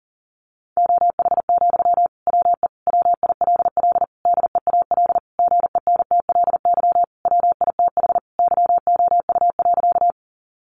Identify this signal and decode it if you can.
Morse code, heard as a keyed tone, about 40 wpm, 700 Hz. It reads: OH, WE WILL DEAL GENTLY WITH YOU.
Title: Sentences from Top 1500 Words 40wpm